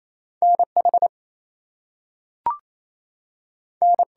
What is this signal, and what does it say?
Morse code, heard as a keyed tone, about 28 wpm, 700 Hz. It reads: NH N
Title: US States and Canadian Provinces 28wpm